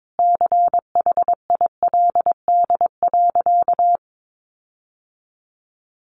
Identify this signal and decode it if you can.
Morse code, heard as a keyed tone, about 22 wpm, 700 Hz. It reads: CHILD.
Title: Sentences from Top 1400 Words 22wpm